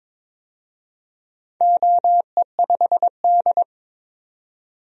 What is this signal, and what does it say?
Morse code, heard as a keyed tone, about 22 wpm, 700 Hz. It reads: OE5D
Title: DXCC-100 Call Signs - 4 Characters - 22wpm